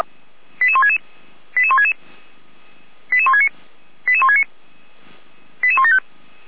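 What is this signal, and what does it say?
Five-tone selective call (ZVEI), heard as quick runs of five tones, tones of about 80 ms. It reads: 88166 88166 88169 88169 88175
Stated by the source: ZVEI five-tone selective call received via SDR.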